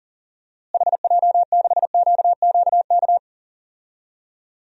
Morse code, heard as a keyed tone, about 40 wpm, 700 Hz. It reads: HJ6QQK
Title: DXCC-100 Call Signs - 6 Characters - Repeated 3x - 40wpm